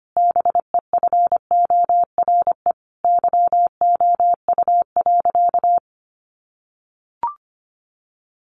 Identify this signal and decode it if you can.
Morse code, heard as a keyed tone, about 25 wpm, 700 Hz. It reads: BEFORE YOU.
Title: Sentences from Top 1400 Words 25wpm